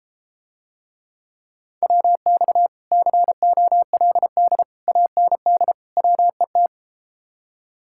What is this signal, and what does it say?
Morse code, heard as a keyed tone, about 33 wpm, 700 Hz. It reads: WX COLD AND WET